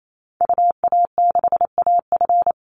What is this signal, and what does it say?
Morse code, heard as a keyed tone, about 28 wpm, 700 Hz. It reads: UA6AF
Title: Call Signs Repeated 3x - 28wpm